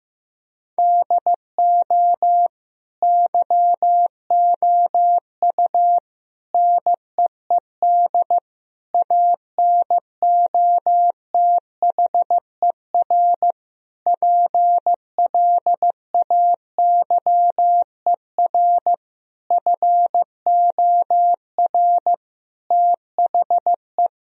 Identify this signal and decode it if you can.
Morse code, heard as a keyed tone, about 16 wpm, 700 Hz. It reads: DO YOU NEED ANOTHER PLAYER FOR THE
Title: Sentences from Top 1000 Words - Review Part 4 - 15wpm